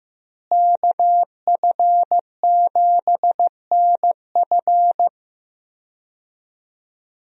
Morse code, heard as a keyed tone, about 15 wpm, 700 Hz. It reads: KF7NF
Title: USA Call Signs - 5 Characters - Repeated 3x - 15wpm